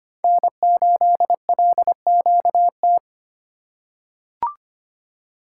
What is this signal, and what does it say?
Morse code, heard as a keyed tone, about 25 wpm, 700 Hz. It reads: N8LQT
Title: USA Call Signs - 5 Characters - 2x Character Spacing - 25wpm